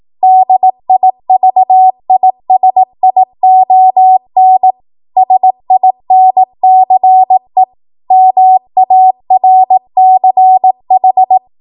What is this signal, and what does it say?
Morse code, heard as a keyed tone, about 18 wpm, 750 Hz. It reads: DIVISION SINCE MARCH
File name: Archive / 18 WPM / 18WPM